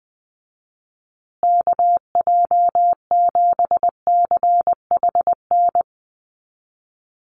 Morse code, heard as a keyed tone, about 20 wpm, 700 Hz. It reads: KJ7CHN
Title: USA Call Signs Repeated 3x 20wpm